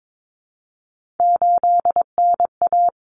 Morse code, heard as a keyed tone, about 22 wpm, 700 Hz. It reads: 8NA
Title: Common Cut Numbers - 3-Digit 22wpm